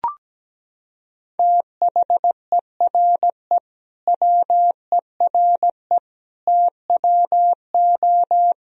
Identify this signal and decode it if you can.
Morse code, heard as a keyed tone, about 17 wpm, 700 Hz. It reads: THERE WERE TWO
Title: Sentences from Top 1000 Words - Review Part 1 - 17wpm